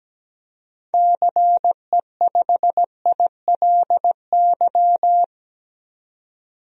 Morse code, heard as a keyed tone, about 17 wpm, 700 Hz. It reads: CE5ILY
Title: DXCC-100 Call Signs - 6 Characters - 17wpm